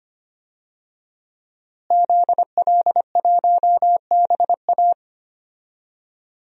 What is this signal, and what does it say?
Morse code, heard as a keyed tone, about 25 wpm, 700 Hz. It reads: ZL1BA
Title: DXCC-100 Call Signs - 5 Characters - Repeated 2x - 25wpm